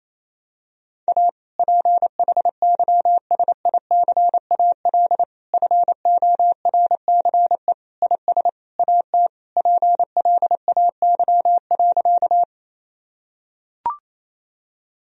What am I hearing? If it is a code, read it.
Morse code, heard as a keyed tone, about 28 wpm, 700 Hz. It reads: A PHYSICAL FORCE IS AT PLAY.
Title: Sentences from Top 1000 Words - Review Part 4 - 28wpm